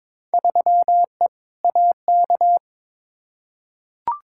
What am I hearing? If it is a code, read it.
Morse code, heard as a keyed tone, about 22 wpm, 700 Hz. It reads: 3E AK
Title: Field Day Exchanges Encore - 22wpm